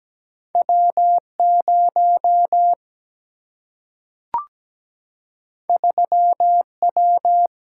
Morse code, heard as a keyed tone, about 17 wpm, 700 Hz. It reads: W0 3W
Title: Single-Letter Single-Number 17wpm